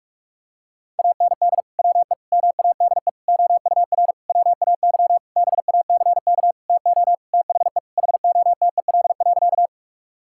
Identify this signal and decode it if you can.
Morse code, heard as a keyed tone, about 46 wpm, 700 Hz. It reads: AND WE MADE OUR WAY BACK TO THE HOTEL.
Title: Sentences from Top 1600 Words 45wpm